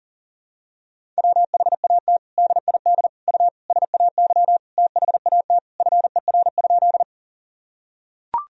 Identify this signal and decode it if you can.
Morse code, heard as a keyed tone, about 40 wpm, 700 Hz. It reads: WHAT DID U SAY THAT FER?
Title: Sentences from Top 200 Words and Abbreviations 40wpm